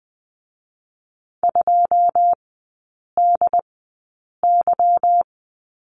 Morse code, heard as a keyed tone, about 20 wpm, 700 Hz. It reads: 2 D Y
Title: 184 Lesson 30e - Letter b - Three Characters 20wpm